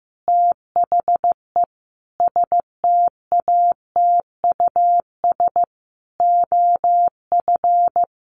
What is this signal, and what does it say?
Morse code, heard as a keyed tone, about 15 wpm, 700 Hz. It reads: THE STATUS OF
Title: Sentences from Top 1400 Words 15wpm